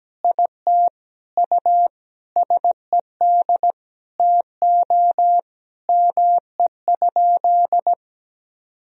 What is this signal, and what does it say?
Morse code, heard as a keyed tone, about 17 wpm, 700 Hz. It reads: IT USED TO ME?
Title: Sentences from Top 200 Words and Abbreviations 17wpm